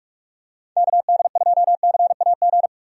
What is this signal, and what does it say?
Morse code, heard as a keyed tone, about 46 wpm, 700 Hz. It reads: KD2CAG
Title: USA Call Signs - 6 Characters - 45wpm